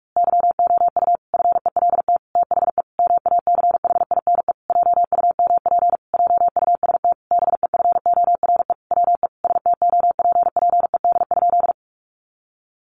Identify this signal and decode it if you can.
Morse code, heard as a keyed tone, about 44 wpm, 700 Hz. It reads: YOU FELT THE MACHINE JUMP JUST BEFORE WE STOPPED?
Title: A Sound Of Thunder 45wpm